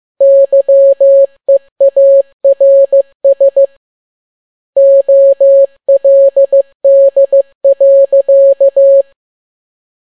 Morse code, heard as a keyed tone, about 15 wpm, 550 Hz. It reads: YEARS OLD.
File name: n6tt p1 15C13